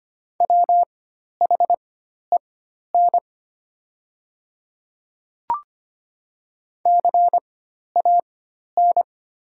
Morse code, heard as a keyed tone, about 25 wpm, spaced as for 12 wpm, 700 Hz. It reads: WHEN CAN
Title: 4x Character Spacing - Top 100 Words - 25wpm